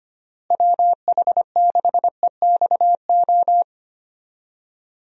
Morse code, heard as a keyed tone, about 25 wpm, 700 Hz. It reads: WH6EXO
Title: USA Call Signs - 6 Characters - Repeated 3x - 25wpm